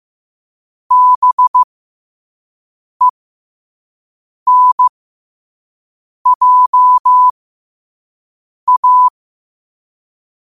Morse code, heard as a keyed tone, about 15 wpm, spaced as for 5 wpm, 1000 Hz. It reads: BENJA